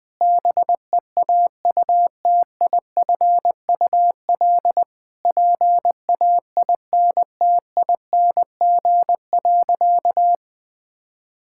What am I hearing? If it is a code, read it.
Morse code, heard as a keyed tone, about 20 wpm, 700 Hz. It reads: BEAUTIFUL PAINTING.
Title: Sentences from Top 1000 Words - Review Part 6 - 20wpm